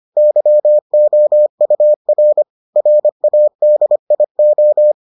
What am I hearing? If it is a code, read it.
Morse code, heard as a keyed tone, about 25 wpm, 600 Hz. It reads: YOUR RADIO